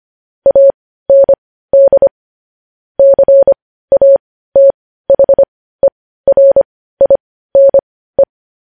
Morse code, heard as a keyed tone, about 25 wpm, spaced as for 15 wpm, 550 Hz. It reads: AND CATHERINE
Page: news headlines in morse code at 15 wpm